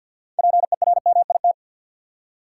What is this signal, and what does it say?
Morse code, heard as a keyed tone, about 53 wpm, 700 Hz. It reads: PERMIT
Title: Six Letter Words - 50wpm